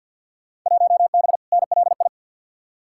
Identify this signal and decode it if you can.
Morse code, heard as a keyed tone, about 50 wpm, 700 Hz. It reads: 1B NLI
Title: Field Day Exchanges - 50wpm